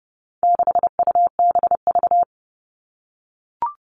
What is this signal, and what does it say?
Morse code, heard as a keyed tone, about 30 wpm, 700 Hz. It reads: 6UBV
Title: All Cut Numbers - 4-Digit 30wpm